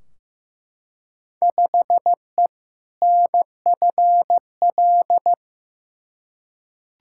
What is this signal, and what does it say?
Morse code, heard as a keyed tone, about 15 wpm, 700 Hz. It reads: 5E NFL
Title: Field Day Exchanges Encore - 15wpm